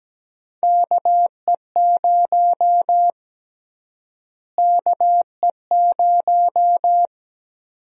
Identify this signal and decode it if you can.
Morse code, heard as a keyed tone, about 17 wpm, 700 Hz. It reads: KE0 KE0
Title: USA Call Sign Prefixes - Repeated 2x - 17wpm